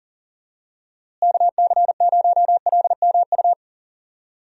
Morse code, heard as a keyed tone, about 40 wpm, 700 Hz. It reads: KC0LMU
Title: USA Call Signs - 6 Characters - Repeated 3x - 40wpm